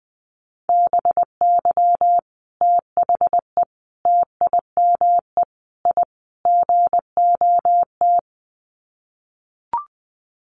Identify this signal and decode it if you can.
Morse code, heard as a keyed tone, about 20 wpm, 700 Hz. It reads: BY THE TIME I GOT
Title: Sets of 5 Words Encore2 20wpm